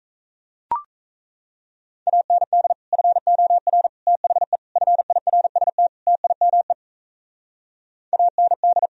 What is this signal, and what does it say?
Morse code, heard as a keyed tone, about 37 wpm, 700 Hz. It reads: AND FOR THE FIRST TIME AND